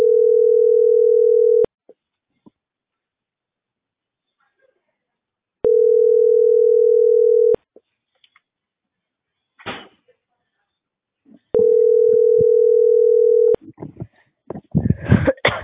Ringback tone (North America), 440 + 480 Hz.